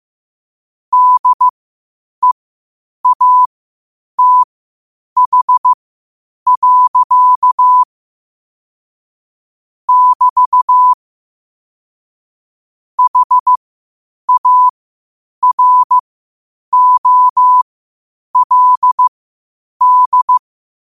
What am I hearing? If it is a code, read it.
Morse code, heard as a keyed tone, about 15 wpm, spaced as for 8 wpm, 1000 Hz. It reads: DEATH. <BT> HAROLD
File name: qotdfiles / 8wpm